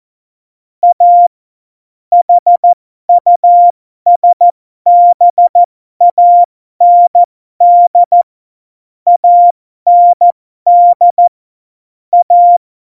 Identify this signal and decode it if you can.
Morse code, heard as a keyed tone, about 14 wpm, spaced as for 12 wpm, 700 Hz. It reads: A HUSBAND AND A
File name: Morsekurs / Witze